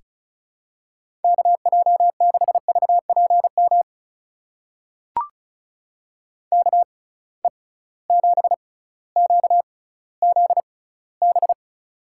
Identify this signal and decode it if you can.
Morse code, heard as a keyed tone, about 36 wpm, spaced as for 18 wpm, 700 Hz. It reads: KJ6VPM KE7QZB
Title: USA Call Signs - 6 Characters - 6x Character Spacing - 35wpm